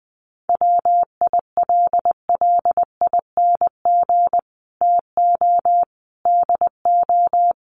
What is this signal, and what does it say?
Morse code, heard as a keyed tone, about 19 wpm, 700 Hz. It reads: WILLING TO DO
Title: Sentences from Top 1400 Words 20wpm